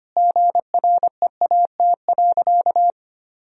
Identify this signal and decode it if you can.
Morse code, heard as a keyed tone, about 25 wpm, 700 Hz. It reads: GREAT.
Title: Sentences from Top 1000 Words - Review Part 2 - 25wpm